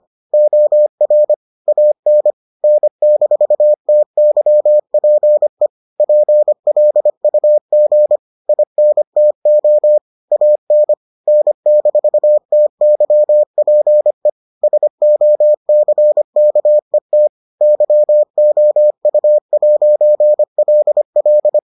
Morse code, heard as a keyed tone, about 25 wpm, 600 Hz. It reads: OR AN N-TYPE PLUG INTO AN N-TYPE SOCKET YOU'LL